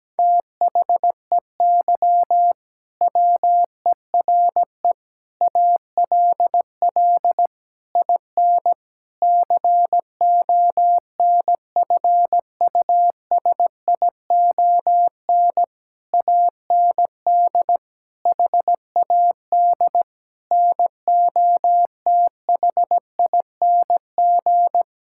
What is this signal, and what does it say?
Morse code, heard as a keyed tone, about 17 wpm, 700 Hz. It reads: THEY WERE ALL IN CONFUSION AND HAD NOTHING
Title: Sentences from Top 1600 Words 17wpm